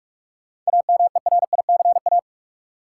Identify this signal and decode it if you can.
Morse code, heard as a keyed tone, about 43 wpm, 700 Hz. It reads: AMERICA
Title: Sets of 5 Words - New Word Drill - 45wpm